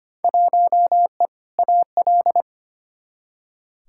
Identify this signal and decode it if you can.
Morse code, heard as a keyed tone, about 25 wpm, 700 Hz. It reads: 1E AL
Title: Field Day Exchanges - 25wpm